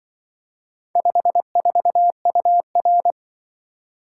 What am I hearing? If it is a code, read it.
Morse code, heard as a keyed tone, about 24 wpm, 700 Hz. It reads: 54UR
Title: Speed-Racing - Four Letter-Number Combinations - 20wpm